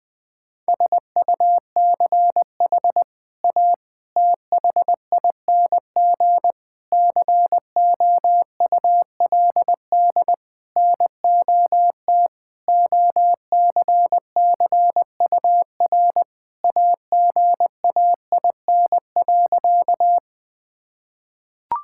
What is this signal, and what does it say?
Morse code, heard as a keyed tone, about 20 wpm, 700 Hz. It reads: SUCH A THING COULD NOT OCCUR AGAIN.
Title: Sentences from Top 1000 Words - Review Part 6 - 20wpm